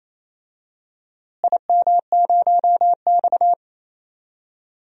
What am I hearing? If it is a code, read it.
Morse code, heard as a keyed tone, about 28 wpm, 700 Hz. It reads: IM0X
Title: DXCC-100 Call Signs - 4 Characters - Repeated 2x - 28wpm